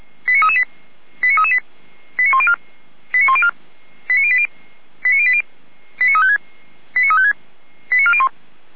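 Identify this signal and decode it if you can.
Five-tone selective call (ZVEI), heard as quick runs of five tones, tones of about 70 ms. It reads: 89338 89338 89104 89104 89080 89080 89356 89356 89401
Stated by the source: ZVEI five-tone call received on an SDR.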